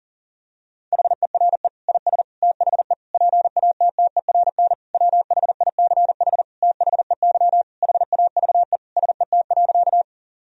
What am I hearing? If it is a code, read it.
Morse code, heard as a keyed tone, about 40 wpm, 700 Hz. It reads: HERE IS THE PATTERN WHICH THEY HAVE SET.
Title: Sentences from Top 1000 Words - Review Part 2 - 40wpm